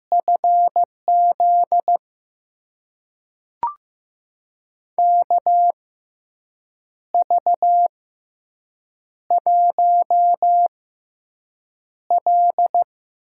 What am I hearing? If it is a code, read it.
Morse code, heard as a keyed tone, about 15 wpm, spaced as for 6 wpm, 700 Hz. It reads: FZ KV1L
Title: USA Call Signs - 4 Characters - 6x Character Spacing - 15wpm